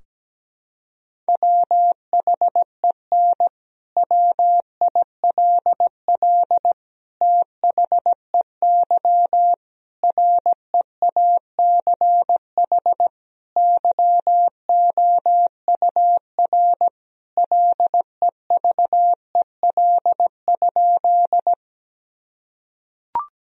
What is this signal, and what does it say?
Morse code, heard as a keyed tone, about 17 wpm, 700 Hz. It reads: WHEN WILL THEY REACH YOUR LEVEL?
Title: Sentences from Top 1000 Words - Review Part 4 - 17wpm